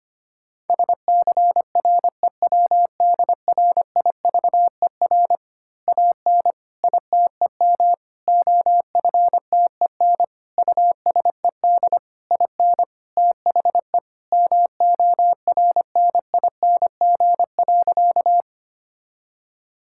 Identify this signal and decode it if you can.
Morse code, heard as a keyed tone, about 25 wpm, 700 Hz. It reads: SCREWDRIVER AN ITEM OFTEN USED IN THE MORNING.
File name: practice / 25wpm / Easy Words with Hints - 25wpm